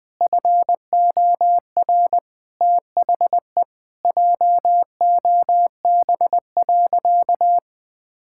Morse code, heard as a keyed tone, about 21 wpm, 700 Hz. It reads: FOR THE JOB.
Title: Sentences from Top 1600 Words 20wpm